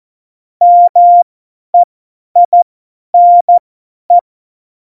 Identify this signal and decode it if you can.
Morse code, heard as a keyed tone, about 14 wpm, spaced as for 10 wpm, 700 Hz. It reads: MEINE
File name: Morsekurs / Witze / Witz 03.Dezember 2019, 50 F70